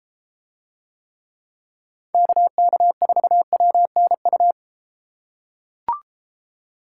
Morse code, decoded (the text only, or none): KK4WNU